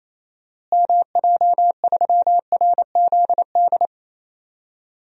Morse code, heard as a keyed tone, about 28 wpm, 700 Hz. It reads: MJ3RZD
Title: DXCC-100 Call Signs - 6 Characters - Repeated 3x - 28wpm